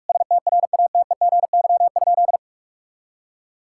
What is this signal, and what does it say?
Morse code, heard as a keyed tone, about 46 wpm, 700 Hz. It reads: STRATEGY?
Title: Sentences from Top 1000 Words - Review Part 6 - 45wpm